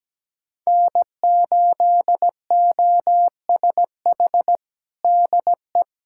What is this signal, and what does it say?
Morse code, heard as a keyed tone, about 17 wpm, 700 Hz. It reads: N8OSH DE
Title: USA Callsign DE Callsign - 5 Characters - 17wpm